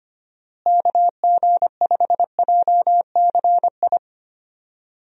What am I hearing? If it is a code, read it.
Morse code, heard as a keyed tone, about 25 wpm, 700 Hz. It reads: KG5JCI
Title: USA Call Signs - 6 Characters - Repeated 3x - 25wpm